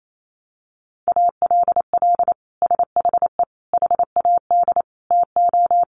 Morse code, heard as a keyed tone, about 28 wpm, 700 Hz. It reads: ALL SHE HAD TO
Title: Sentences from Top 1100 Words 28wpm